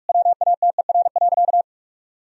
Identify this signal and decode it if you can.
Morse code, heard as a keyed tone, about 46 wpm, 700 Hz. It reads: WATER.
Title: Sentences from Top 1000 Words - Review Part 1 - 45wpm